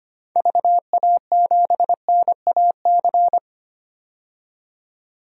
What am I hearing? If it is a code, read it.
Morse code, heard as a keyed tone, about 25 wpm, 700 Hz. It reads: VA7NAC
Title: DXCC-100 Call Signs - 6 Characters - 25wpm